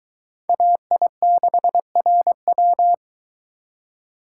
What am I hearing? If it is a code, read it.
Morse code, heard as a keyed tone, about 23 wpm, 700 Hz. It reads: AI6RW